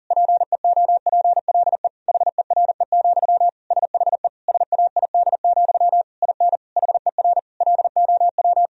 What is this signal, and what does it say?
Morse code, heard as a keyed tone, about 40 wpm, 700 Hz. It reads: PEOPLE HERE, SHE SAID, IN HER LOW
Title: Sentences from Top 1000 Words - Review Part 1 - 40wpm